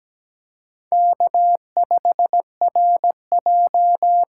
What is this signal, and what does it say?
Morse code, heard as a keyed tone, about 17 wpm, 700 Hz. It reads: K5RJ
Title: USA Call Signs - 5 Characters - 6x Character Spacing - 17wpm